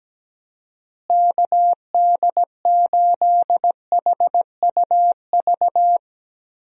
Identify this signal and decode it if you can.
Morse code, heard as a keyed tone, about 17 wpm, 700 Hz. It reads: KD8HUV
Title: USA Call Signs - 6 Characters - Repeated 2x - 17wpm